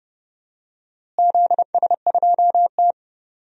Morse code, heard as a keyed tone, about 28 wpm, 700 Hz. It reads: ZS2T
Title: DXCC-100 Call Signs - 4 Characters - Repeated 3x - 30wpm